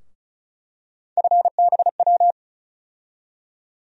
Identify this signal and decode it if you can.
Morse code, heard as a keyed tone, about 36 wpm, 700 Hz. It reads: FBW
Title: USA Call Sign Postfixes - 3 Letters 35wpm